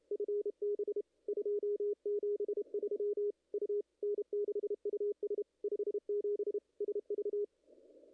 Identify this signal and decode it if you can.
Morse code, heard as a keyed tone, about 30 wpm, 410 Hz. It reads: FB 273 U N6US 57 SV
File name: contests / ss-cw-k5tr-op-n5rz -0100